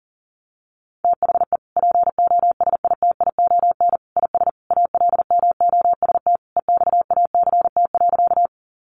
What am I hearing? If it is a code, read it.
Morse code, heard as a keyed tone, about 40 wpm, 700 Hz. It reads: THE POSITION IS ALMOST EXACT.
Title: Sentences from Top 1300 Words 40wpm